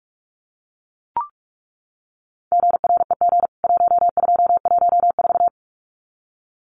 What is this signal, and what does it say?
Morse code, heard as a keyed tone, about 44 wpm, 700 Hz. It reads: GREG 1214
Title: CWT Exchange 45wpm